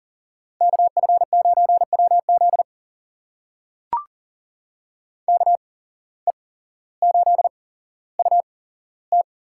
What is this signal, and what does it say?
Morse code, heard as a keyed tone, about 38 wpm, spaced as for 17 wpm, 700 Hz. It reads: KF9WZ KE8UT
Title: USA Call Signs - 5 Characters - 8x Character Spacing - 40wpm